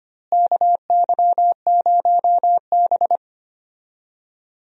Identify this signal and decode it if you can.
Morse code, heard as a keyed tone, about 25 wpm, 700 Hz. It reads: KY0B
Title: USA Call Signs - 4 Characters - Repeated 3x - 25wpm